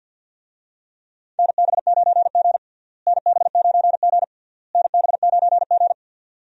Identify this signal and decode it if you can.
Morse code, heard as a keyed tone, about 51 wpm, 700 Hz. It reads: NB9G NB9G NB9G
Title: USA Call Signs - 4 Characters - Repeated 3x - 50wpm